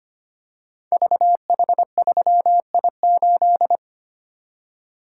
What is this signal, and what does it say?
Morse code, heard as a keyed tone, about 25 wpm, 700 Hz. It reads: VH3I8
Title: Code Groups - 5 Letter:Numbers - 25wpm